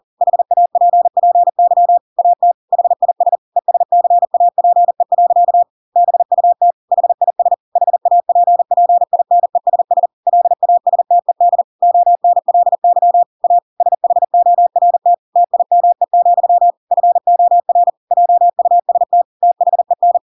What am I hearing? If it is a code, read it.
Morse code, heard as a keyed tone, about 40 wpm, 700 Hz. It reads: HAPPY AT HIS ESCAPE. BUT HIS HAPPINESS LASTED ONLY A SHORT TIME, FOR JUST THEN
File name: Pinocchio / Pinocchio ch4 40 40 ik1hge